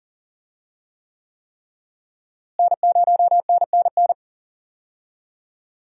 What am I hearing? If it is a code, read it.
Morse code, heard as a keyed tone, about 40 wpm, 700 Hz. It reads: N0NNN
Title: USA Call Signs - 5 Characters - Repeated 3x - 40wpm